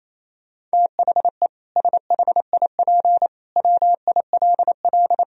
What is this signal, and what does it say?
Morse code, heard as a keyed tone, about 28 wpm, 700 Hz. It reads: THE SHIP WILL
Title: Sentences from Top 1000 Words - Review Part 4 - 28wpm